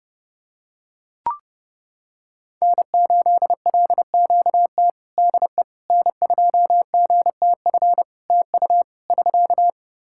Morse code, heard as a keyed tone, about 30 wpm, 700 Hz. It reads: N8LQT DE N2GTF TU <SK>
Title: K1USN SST QSO Practice 30wpm